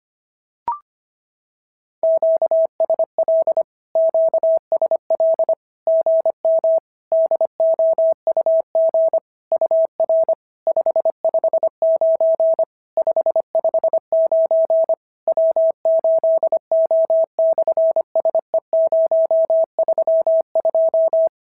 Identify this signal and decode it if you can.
Morse code, heard as a keyed tone, about 25 wpm, 650 Hz. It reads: QSL QSL GM DOUG UR 559 559 W8O/SE032
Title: SOTA QSO Practice 25wpm